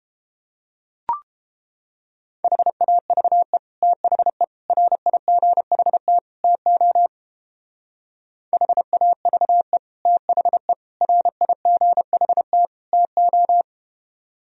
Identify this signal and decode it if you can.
Morse code, heard as a keyed tone, about 31 wpm, 700 Hz. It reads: HAVE THE RIGHT TO HAVE THE RIGHT TO